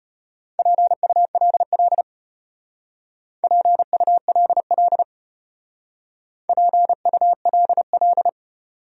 Morse code, heard as a keyed tone, about 32 wpm, 700 Hz. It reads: PULL PULL PULL